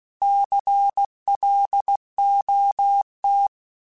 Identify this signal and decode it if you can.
Morse code, heard as a keyed tone, about 16 wpm, 800 Hz. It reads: CLOT
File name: Alphabetical / Alphabetical mp3s / Cs @15